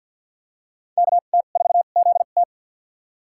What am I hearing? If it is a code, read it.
Morse code, heard as a keyed tone, about 53 wpm, spaced as for 37 wpm, 700 Hz. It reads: KT4ZT
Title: USA Call Signs - 5 Characters - 2x Character Spacing - 50wpm